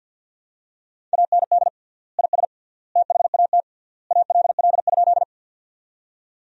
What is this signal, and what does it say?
Morse code, heard as a keyed tone, about 53 wpm, 700 Hz. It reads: AND IS THAT ALL?
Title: 3x Word-Spacing - Sentences from Top 100 Words - 50wpm